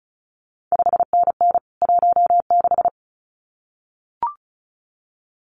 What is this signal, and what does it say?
Morse code, heard as a keyed tone, about 36 wpm, 700 Hz. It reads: 5NN 16
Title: CQ WW DX Contest Exchange - 35wpm